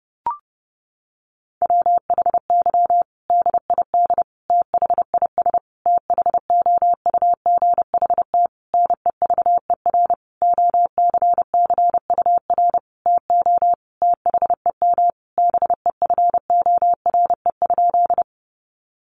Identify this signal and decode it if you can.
Morse code, heard as a keyed tone, about 30 wpm, 700 Hz. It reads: WHY DID THIS THOUGHT NEVER OCCUR TO THEM BEFORE?
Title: Sentences from Top 900 Words 30wpm